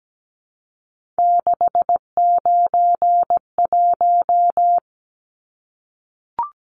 Morse code, heard as a keyed tone, about 17 wpm, 700 Hz. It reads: 691
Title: Numbers - 3-Digit 17wpm